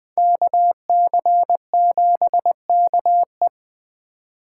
Morse code, heard as a keyed tone, about 24 wpm, 700 Hz. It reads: KC7KE
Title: Speed Racing - USA 2x2 Call Signs 20wpm